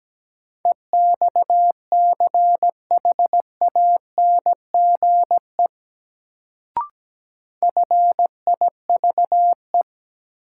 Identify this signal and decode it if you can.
Morse code, heard as a keyed tone, about 17 wpm, 700 Hz. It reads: EXCHANGE FIVE
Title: Sets of 5 Words - New Word Drill - 17wpm